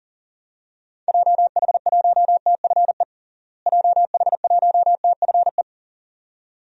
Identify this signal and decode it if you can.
Morse code, heard as a keyed tone, about 40 wpm, 700 Hz. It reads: JH1TFE JH1TFE
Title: DXCC-100 Call Signs - 6 Characters - Repeated 2x - 40wpm